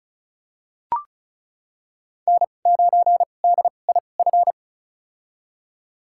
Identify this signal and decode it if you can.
Morse code, heard as a keyed tone, about 36 wpm, spaced as for 25 wpm, 700 Hz. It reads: N9DIF
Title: USA Call Signs - 5 Characters - 2x Character Spacing - 35wpm